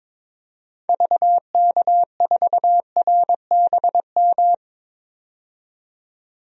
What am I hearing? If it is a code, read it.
Morse code, heard as a keyed tone, about 22 wpm, 700 Hz. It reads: VK4RBM
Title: DXCC-100 Call Signs - 6 Characters - 22wpm